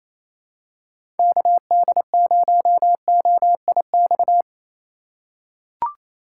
Morse code, heard as a keyed tone, about 28 wpm, 700 Hz. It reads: KD0OIX
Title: K1USN SST QSO Practice - 28wpm